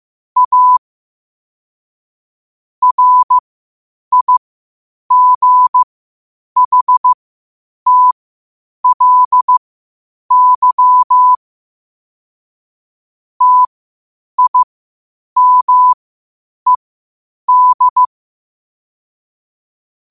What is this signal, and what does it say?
Morse code, heard as a keyed tone, about 15 wpm, spaced as for 8 wpm, 1000 Hz. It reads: A RIGHTLY TIMED
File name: qotdfiles / 8wpm